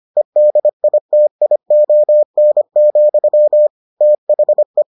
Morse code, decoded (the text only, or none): EDITION, THE